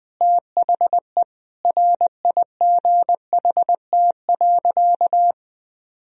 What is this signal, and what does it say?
Morse code, heard as a keyed tone, about 20 wpm, 700 Hz. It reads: THE RIGHT.
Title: Sentences from Top 1000 Words - Review Part 1 - 20wpm